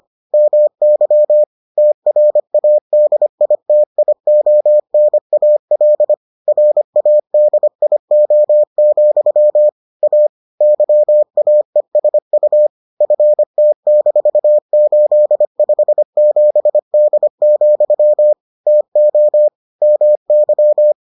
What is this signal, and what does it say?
Morse code, heard as a keyed tone, about 25 wpm, 600 Hz. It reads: MY TRADITIONAL RADIO, A YAESU FT-857D, TO MY